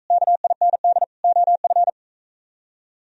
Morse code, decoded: KIND OF

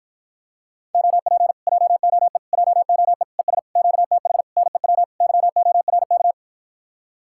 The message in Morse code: OP JOE JOE ES QTH NEW YORK

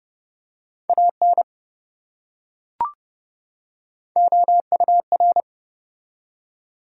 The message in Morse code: AN OUR

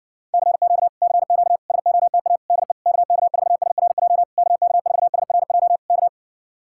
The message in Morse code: CQ CQ SOTA DE KK4SRJ KK4SRJ K

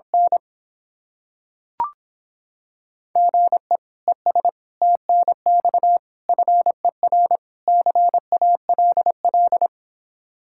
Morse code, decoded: N GE ES TNX FER CALL

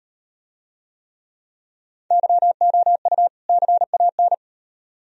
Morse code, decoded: YOU CAN

